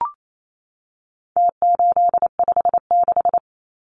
Morse code, heard as a keyed tone, about 28 wpm, 700 Hz. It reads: T856
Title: Numbers - Contest Style - 4-Digit 28wpm